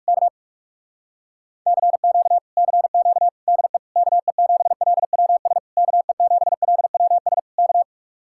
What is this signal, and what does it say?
Morse code, heard as a keyed tone, about 58 wpm, 700 Hz. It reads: K CQ CQ DE KE7LWS KE7LWS K